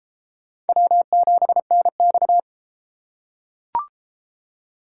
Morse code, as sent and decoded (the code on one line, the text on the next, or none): .-- --... -. -..-
W7NX